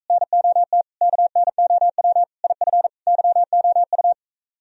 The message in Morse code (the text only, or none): NOT KNOW IF YOU